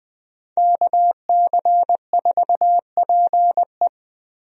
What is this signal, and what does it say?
Morse code, heard as a keyed tone, about 21 wpm, 700 Hz. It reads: KC4PE
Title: USA Call Signs - 5 Characters - Repeated 2x - 20wpm